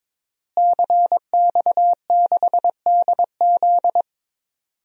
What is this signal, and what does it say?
Morse code, heard as a keyed tone, about 22 wpm, 700 Hz. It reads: CX6DZ
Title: DXCC-100 Call Signs - 5 Characters - Repeated 3x - 22wpm